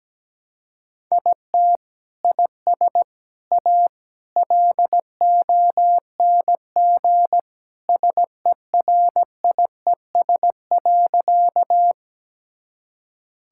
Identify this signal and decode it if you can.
Morse code, heard as a keyed tone, about 17 wpm, 700 Hz. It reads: IT IS A LONG SERIES.
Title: Sentences from Top 1000 Words - Review Part 5 - 17wpm